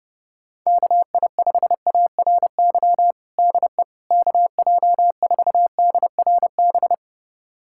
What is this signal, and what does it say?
Morse code, heard as a keyed tone, about 30 wpm, 700 Hz. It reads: KI5ARY DE KJ4DRB
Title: USA Callsign DE Callsign 30wpm